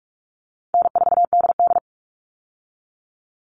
Morse code, heard as a keyed tone, about 44 wpm, 700 Hz. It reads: N4DD
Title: All Cut Numbers - 4-Digit 45wpm